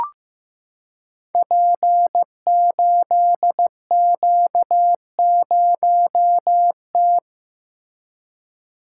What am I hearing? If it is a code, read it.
Morse code, heard as a keyed tone, about 15 wpm, 700 Hz. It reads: P8Q0T
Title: Code Groups - 5 Letter:Numbers - 15wpm